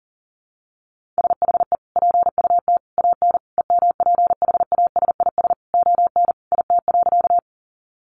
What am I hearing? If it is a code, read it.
Morse code, heard as a keyed tone, about 37 wpm, 700 Hz. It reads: SHE PUT AN EMPHASIS ON IT.
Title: Sentences from Top 1400 Words 40wpm